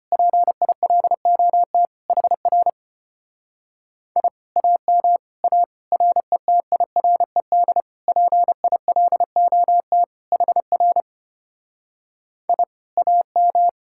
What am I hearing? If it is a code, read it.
Morse code, heard as a keyed tone, about 32 wpm, 700 Hz. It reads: PILOT HR I AM A RETIRED PILOT HR I AM